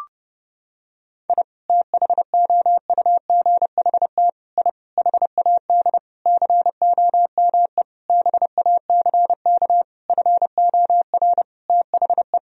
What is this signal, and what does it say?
Morse code, heard as a keyed tone, about 30 wpm, 700 Hz. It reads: I THOUGHT I HAD COME BACK FOR THE
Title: Sentences from Top 1000 Words - Review Part 4 - 30wpm